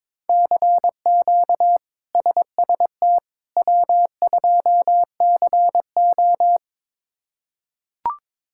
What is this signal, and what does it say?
Morse code, heard as a keyed tone, about 22 wpm, 700 Hz. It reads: CQ SST W2CO
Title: K1USN SST QSO Practice - 22wpm